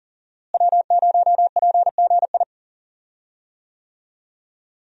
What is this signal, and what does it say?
Morse code, heard as a keyed tone, about 37 wpm, 700 Hz. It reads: W0PGI
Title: USA Call Signs - 5 Characters - Repeated 3x - 40wpm